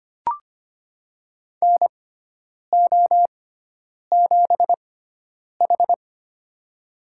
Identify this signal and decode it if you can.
Morse code, heard as a keyed tone, about 25 wpm, spaced as for 9 wpm, 700 Hz. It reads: NO7H
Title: USA Call Signs - 6x Character Spacing - 25wpm